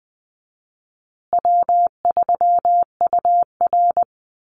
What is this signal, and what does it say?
Morse code, heard as a keyed tone, about 20 wpm, 700 Hz. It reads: W3UR
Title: Call Signs Repeated 3x - 20wpm